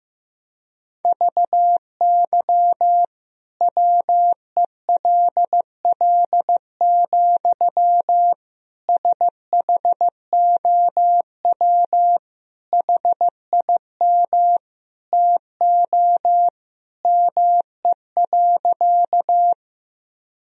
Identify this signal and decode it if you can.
Morse code, heard as a keyed tone, about 15 wpm, 700 Hz. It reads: VY WELL, SHOW HIM TO ME.
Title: Sentences from Top 200 Words and Abbreviations 15wpm